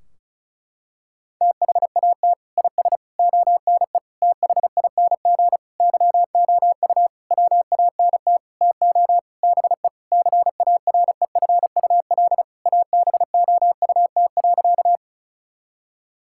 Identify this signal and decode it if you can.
Morse code, heard as a keyed tone, about 34 wpm, 700 Hz. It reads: THAT IS ONE THING YOU WANT TO BE CAREFUL ABOUT.
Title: Sentences from Top 1600 Words 35wpm